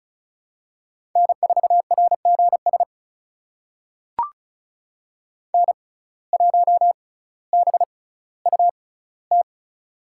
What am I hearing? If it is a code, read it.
Morse code, heard as a keyed tone, about 36 wpm, spaced as for 18 wpm, 700 Hz. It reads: N4RGS N1BUT
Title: USA Call Signs - 5 Characters - 6x Character Spacing - 35wpm